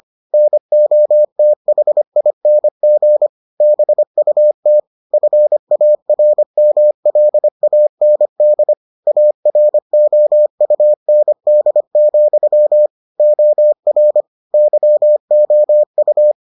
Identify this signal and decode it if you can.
Morse code, heard as a keyed tone, about 25 wpm, 600 Hz. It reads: NOTHING BUT FARMLAND AROUND, OR YOU